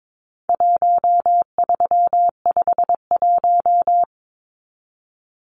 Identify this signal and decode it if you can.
Morse code, heard as a keyed tone, about 22 wpm, 700 Hz. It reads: 1351
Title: Numbers - 4-Digit 22wpm